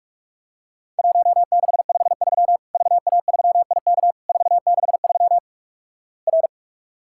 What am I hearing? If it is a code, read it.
Morse code, heard as a keyed tone, about 46 wpm, 700 Hz. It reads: 1653 VA3IK 463 R
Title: Worked All Europe DX Contest - QTC Exchange - 45wpm